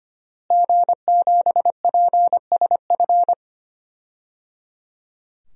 Morse code, heard as a keyed tone, about 25 wpm, 700 Hz. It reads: G7PSF
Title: DXCC-100 Call Signs - 5 Characters - Repeated 3x - 25wpm